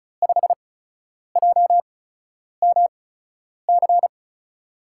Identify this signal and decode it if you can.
Morse code, heard as a keyed tone, about 36 wpm, spaced as for 10 wpm, 700 Hz. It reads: 5JMC
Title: USA Call Signs - 5 Characters - 8x Character Spacing - 35wpm